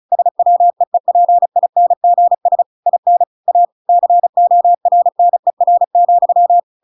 Morse code, heard as a keyed tone, about 34 wpm, 700 Hz. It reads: SWEEPINGS IN A CORNER,